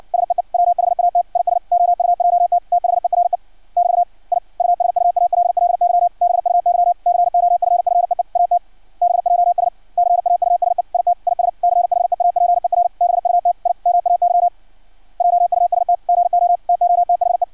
Morse code, heard as a keyed tone, about 56 wpm, 700 Hz. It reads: SEE WHAT IS OUJT THERE <BT> I USUALLY BUY COFFEE AT BJS CAUSE IT IS CHEAPER BUT I MAY JUST GO TO THE
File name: archived / CRSnet